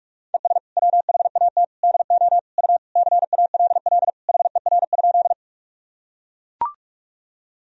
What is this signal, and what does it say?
Morse code, heard as a keyed tone, about 41 wpm, 700 Hz. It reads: ES WHAT DO U CALL HER?
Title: Sentences from Top 100 Words and Abbreviations 45wpm